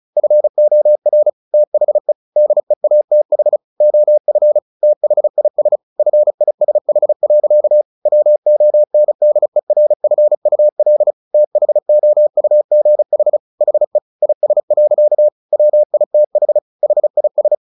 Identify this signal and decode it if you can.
Morse code, heard as a keyed tone, about 34 wpm, 600 Hz. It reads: FOR THE DEATH OF THIS FISH. WONDERFUL THOUGH HE IS. WITH HIS
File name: the old man and the sea / the old man and the sea 1 35wpm